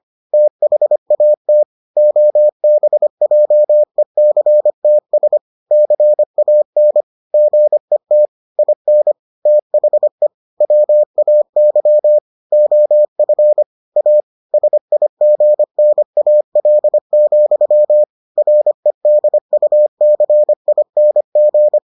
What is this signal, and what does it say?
Morse code, heard as a keyed tone, about 25 wpm, 600 Hz. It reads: THAT OBJECTS CAN GET IN THE WAY OF A SIGNAL, REDUCING